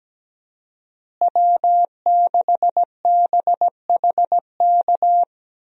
Morse code, heard as a keyed tone, about 17 wpm, 700 Hz. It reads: W6BHK
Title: USA Call Signs - 5 Characters - Repeated 2x - 17wpm